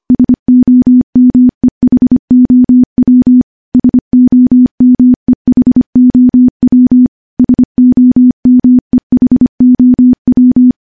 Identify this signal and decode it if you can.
Morse code, heard as a keyed tone, about 25 wpm, 250 Hz. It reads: SOMEHOW SOMEHOW SOMEHOW
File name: morse-training-25WPM